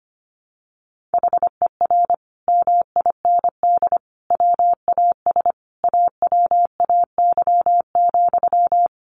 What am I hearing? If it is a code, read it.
Morse code, heard as a keyed tone, about 25 wpm, 700 Hz. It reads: HER MIND WAS AWAY,
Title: Sentences from Top 1500 Words 25wpm